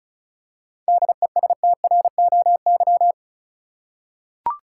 Morse code, heard as a keyed tone, about 34 wpm, 700 Hz. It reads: DESTROY
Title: Seven Letter Words - 35wpm